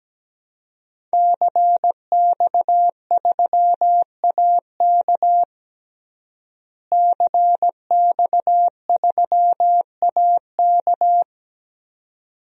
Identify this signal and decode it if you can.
Morse code, heard as a keyed tone, about 17 wpm, 700 Hz. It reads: CX3AK CX3AK
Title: DXCC-100 Call Signs - 5 Characters - Repeated 2x - 17wpm